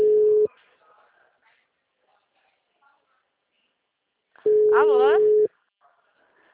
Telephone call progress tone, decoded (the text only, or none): ringback tone